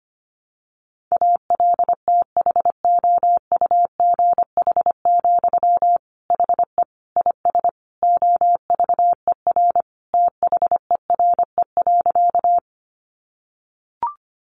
Morse code, heard as a keyed tone, about 25 wpm, 700 Hz. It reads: ALTHOUGH, HE IS OVER THERE.
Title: Sentences from Top 600 Words 25wpm